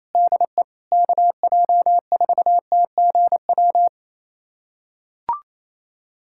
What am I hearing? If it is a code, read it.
Morse code, heard as a keyed tone, about 28 wpm, 700 Hz. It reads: DE KJ4TGW
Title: USA Callsign DE Callsign - 6 Characters - 28wpm